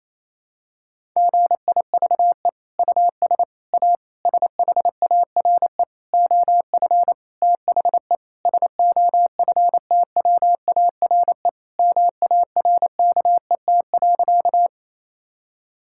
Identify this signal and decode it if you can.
Morse code, heard as a keyed tone, about 28 wpm, 700 Hz. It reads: GIVE US A SHARE OF THE SOFTWARE MARKET.
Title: Sentences from Top 1000 Words - Review Part 6 - 28wpm